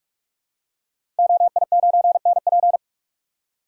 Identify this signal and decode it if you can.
Morse code, heard as a keyed tone, about 44 wpm, 700 Hz. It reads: OI9NP
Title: Code Groups - 5 Letter:Numbers - 45wpm